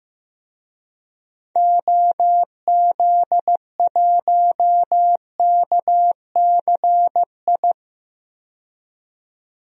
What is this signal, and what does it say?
Morse code, heard as a keyed tone, about 15 wpm, 690 Hz. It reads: OZ1KCI
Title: DXCC-100 Call Signs - 6 Characters - 15wpm